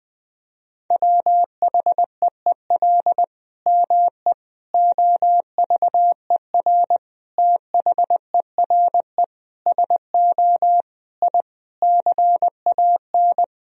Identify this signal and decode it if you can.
Morse code, heard as a keyed tone, about 20 wpm, 700 Hz. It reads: WHEEL ME OVER THERE SO I CAN
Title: Sentences from Top 1000 Words - Review Part 3 - 20wpm